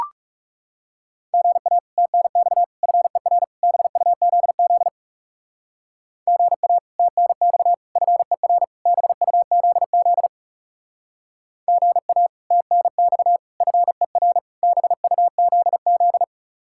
Morse code, heard as a keyed tone, about 36 wpm, 700 Hz. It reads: GA TNX FER BUZZ GA TNX FER BUZZ GA TNX FER BUZZ
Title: Speed-Racing Common QSO Phrases Encore - 30wpm